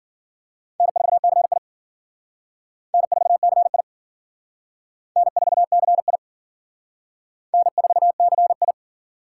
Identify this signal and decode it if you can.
Morse code, heard as a keyed tone, about 51 wpm, 700 Hz. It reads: N4CI N4CI N4CI N4CI